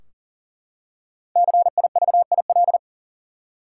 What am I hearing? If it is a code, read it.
Morse code, heard as a keyed tone, about 40 wpm, 700 Hz. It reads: CIVIL
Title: Five Letter Words - 40wpm